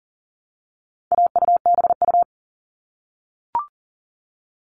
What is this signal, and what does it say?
Morse code, heard as a keyed tone, about 40 wpm, 700 Hz. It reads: AUBU
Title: All Cut Numbers - 4-Digit 40wpm